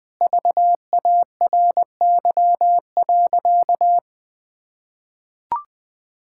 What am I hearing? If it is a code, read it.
Morse code, heard as a keyed tone, about 21 wpm, 700 Hz. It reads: VARY.
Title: Sentences with 7-Word Phrases - Spoken Beginning - 20wpm